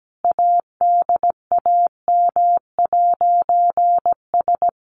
Morse code, heard as a keyed tone, about 17 wpm, 700 Hz. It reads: ADAM'S
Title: A Sound Of Thunder 17wpm